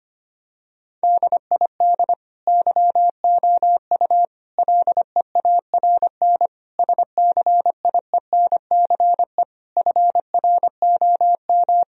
Morse code, heard as a keyed tone, about 25 wpm, 700 Hz. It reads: DID YOU LEARN SCIENCE FROM
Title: Sentences from Top 1000 Words - Review Part 2 - 25wpm